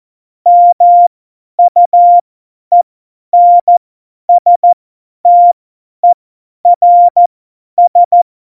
Morse code, heard as a keyed tone, about 14 wpm, spaced as for 10 wpm, 700 Hz. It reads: MUENSTERS
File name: Morsekurs / Witze / Witz 24. November 2020, 50 F70